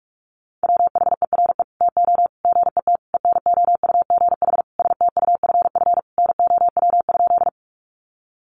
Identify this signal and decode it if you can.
Morse code, heard as a keyed tone, about 44 wpm, 700 Hz. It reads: WHERE TO GET ENOUGH STUFF NOW?
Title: Sentences from Top 1400 Words 45wpm